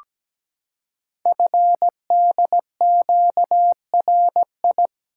Morse code, heard as a keyed tone, about 17 wpm, 700 Hz. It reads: FDQRI
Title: Code Groups - 5 Letters - 17wpm